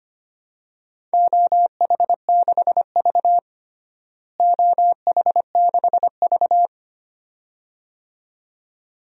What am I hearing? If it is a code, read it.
Morse code, heard as a keyed tone, about 25 wpm, 700 Hz. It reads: OH6V OH6V
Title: DXCC-100 Call Signs - 4 Characters - Repeated 2x - 25wpm